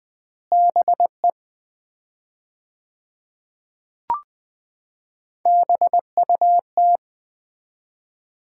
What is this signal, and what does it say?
Morse code, heard as a keyed tone, about 20 wpm, 700 Hz. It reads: BE BUT
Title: 4x Word-Spacing - Sets of 3 Words - 20wpm